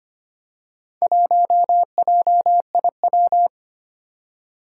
Morse code, heard as a keyed tone, about 25 wpm, 700 Hz. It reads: 1JIW
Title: Four Challenging Letter-Number Combinations 25wpm